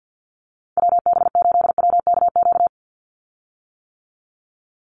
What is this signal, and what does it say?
Morse code, heard as a keyed tone, about 50 wpm, 700 Hz. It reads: WB8WKQ
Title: Call Signs - 50wpm